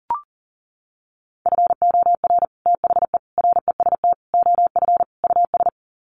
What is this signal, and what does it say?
Morse code, heard as a keyed tone, about 40 wpm, 700 Hz. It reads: FOR THE REST OF US
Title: Sets of 5 Words Encore 40wpm